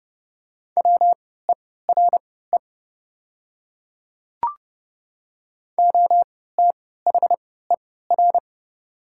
Morse code, heard as a keyed tone, about 30 wpm, spaced as for 17 wpm, 700 Hz. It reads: WERE OTHER
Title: 3x Character Spacing - Top 100 Words - 30wpm